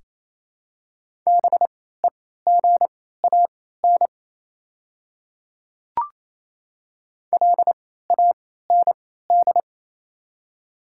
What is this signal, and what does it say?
Morse code, heard as a keyed tone, about 28 wpm, spaced as for 16 wpm, 700 Hz. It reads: BEGAN LAND
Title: 3x Character Spacing - Top 500 Words - 28wpm